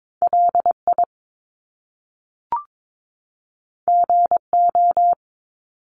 Morse code, heard as a keyed tone, about 22 wpm, 700 Hz. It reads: LI GO
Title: Top Two Letter Combinations in Words 22wpm